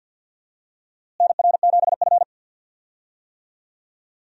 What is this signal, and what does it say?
Morse code, heard as a keyed tone, about 52 wpm, 700 Hz. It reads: NR7F
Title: USA Call Signs - 4 Characters - Repeated 2x - 50wpm